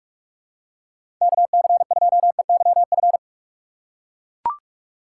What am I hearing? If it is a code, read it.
Morse code, heard as a keyed tone, about 42 wpm, 700 Hz. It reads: KC2EYF
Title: USA Call Signs - 4x Character Spacing - 45wpm